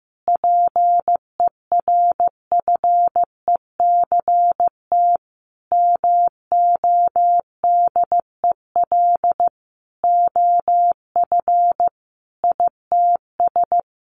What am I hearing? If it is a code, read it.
Morse code, heard as a keyed tone, about 15 wpm, 700 Hz. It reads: PERFECT MODEL OF ITS
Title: Sentences from Top 1000 Words 15wpm